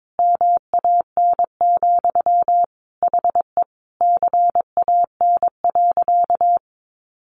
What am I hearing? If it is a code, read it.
Morse code, heard as a keyed tone, about 22 wpm, 700 Hz. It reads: MAN, HE CAN.
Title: Sentences from Top 1500 Words 22wpm